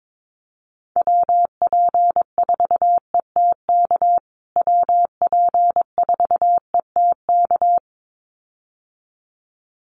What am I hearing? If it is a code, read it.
Morse code, heard as a keyed tone, about 22 wpm, 700 Hz. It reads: WP4ETK WP4ETK
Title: USA Call Signs Repeated 2x 22wpm